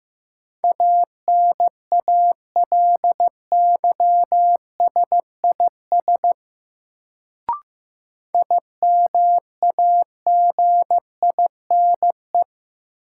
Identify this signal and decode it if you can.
Morse code, heard as a keyed tone, about 15 wpm, 700 Hz. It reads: ANALYSIS IMAGINE
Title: Sentences from Top 1400 Words - New Word Drill Alt - 15wpm